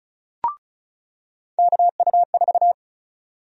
Morse code, heard as a keyed tone, about 36 wpm, 700 Hz. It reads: KU4
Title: USA Call Sign Prefixes - 35wpm